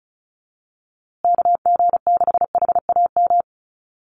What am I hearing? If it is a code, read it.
Morse code, heard as a keyed tone, about 34 wpm, 700 Hz. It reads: KG6HAM
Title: USA Call Signs Repeated 2x 35wpm